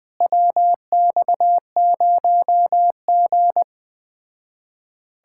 Morse code, heard as a keyed tone, about 20 wpm, 700 Hz. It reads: WX0G
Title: Speed-Racing Common QSO Phrases - 17wpm